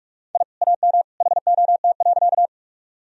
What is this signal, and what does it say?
Morse code, heard as a keyed tone, about 42 wpm, 700 Hz. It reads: I AM HOT.
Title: Sentences from Top 1000 Words - Review Part 3 - 45wpm